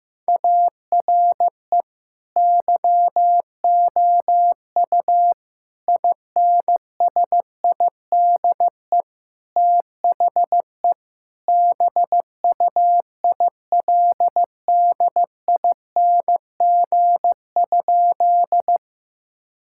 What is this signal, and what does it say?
Morse code, heard as a keyed tone, about 15 wpm, 700 Hz. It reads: ARE YOU INSIDE THE BUILDING?
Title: Sentences from Top 1000 Words - Review Part 6 - 15wpm